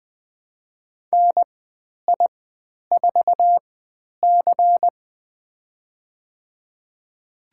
Morse code, heard as a keyed tone, about 20 wpm, spaced as for 10 wpm, 700 Hz. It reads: NI4C
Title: 102 Lesson 17g - Letter u - Call Signs 20wpm